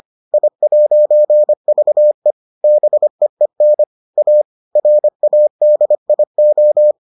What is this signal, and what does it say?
Morse code, heard as a keyed tone, about 25 wpm, 600 Hz. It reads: I'VE BEEN A RADIO